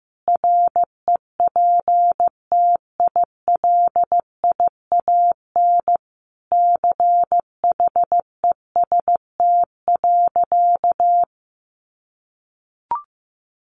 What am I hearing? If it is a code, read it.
Morse code, heard as a keyed tone, about 15 wpm, 700 Hz. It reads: REPTILIAN CHEST.
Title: A Sound Of Thunder 15wpm